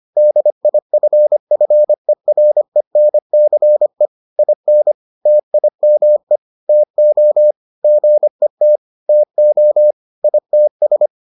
Morse code, heard as a keyed tone, about 25 wpm, 600 Hz. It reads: DIFFERENCE IN TIME TO GET TO ITS